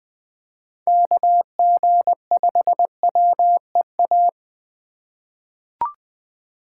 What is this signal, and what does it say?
Morse code, heard as a keyed tone, about 20 wpm, 700 Hz. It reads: KG5WEA
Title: USA Call Signs - 6 Characters - 2x Character Spacing - 20wpm